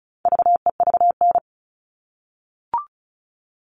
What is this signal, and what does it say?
Morse code, heard as a keyed tone, about 34 wpm, 700 Hz. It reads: VEVN
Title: All Cut Numbers - 4-Digit 35wpm